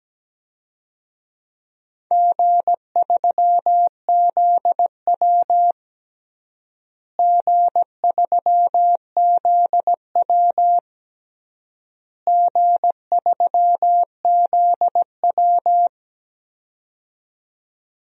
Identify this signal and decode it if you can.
Morse code, heard as a keyed tone, about 17 wpm, 700 Hz. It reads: G3ZW G3ZW G3ZW
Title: DXCC-100 Call Signs - 4 Characters - Repeated 3x - 17wpm